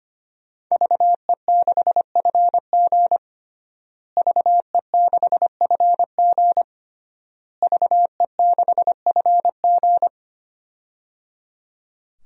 Morse code, heard as a keyed tone, about 25 wpm, 700 Hz. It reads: VE6FG VE6FG VE6FG
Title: DXCC-100 Call Signs - 5 Characters - Repeated 3x - 25wpm